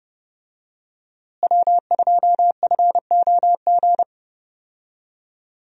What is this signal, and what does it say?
Morse code, heard as a keyed tone, about 30 wpm, 700 Hz. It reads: W2FOG
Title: USA Call Signs - 5 Characters - Repeated 3x - 30wpm